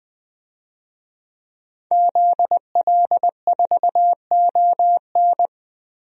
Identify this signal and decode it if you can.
Morse code, heard as a keyed tone, about 20 wpm, 700 Hz. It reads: ZL4ON
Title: DXCC-100 Call Signs - 5 Characters - Repeated 3x - 20wpm